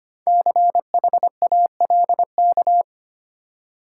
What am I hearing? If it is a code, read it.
Morse code, heard as a keyed tone, about 25 wpm, 700 Hz. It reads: CHALK
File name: practice / 25wpm / Easy Words with Hints - 25wpm